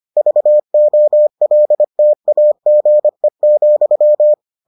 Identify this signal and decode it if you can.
Morse code, heard as a keyed tone, about 25 wpm, 600 Hz. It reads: VOLTAGE,